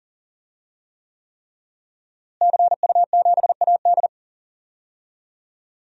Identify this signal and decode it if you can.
Morse code, heard as a keyed tone, about 40 wpm, 700 Hz. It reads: CU7AD
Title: DXCC-100 Call Signs - 5 Characters - Repeated 3x - 40wpm